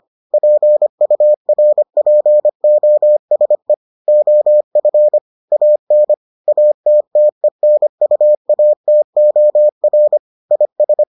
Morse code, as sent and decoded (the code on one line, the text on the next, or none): .--. ..- .-. .--. --- ... / --- ..-. / .- -. / .- - - . -. ..- .- - --- .-. / .. ...
PURPOSE OF AN ATTENUATOR IS